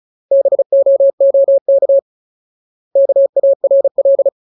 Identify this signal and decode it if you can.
Morse code, heard as a keyed tone, about 34 wpm, 550 Hz. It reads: BOOK KARL
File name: Ultimate Word List / 35WPM Ultimate